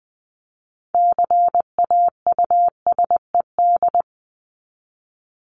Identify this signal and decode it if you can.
Morse code, heard as a keyed tone, about 20 wpm, 700 Hz. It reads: CAUSED
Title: ED Suffix - Top 100 Words with Base Words 20wpm